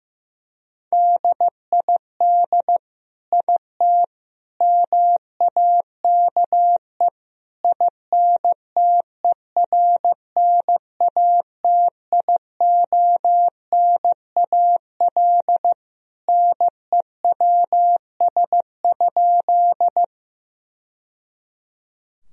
Morse code, heard as a keyed tone, about 15 wpm, 700 Hz. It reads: DID IT MAKE INTERNATIONAL NEWS?
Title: Sentences from Top 1000 Words - Review Part 4 - 15wpm